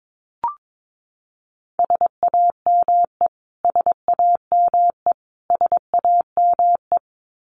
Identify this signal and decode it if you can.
Morse code, heard as a keyed tone, about 22 wpm, 700 Hz. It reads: SAME SAME SAME
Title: Top 200 Words Repeated 22wpm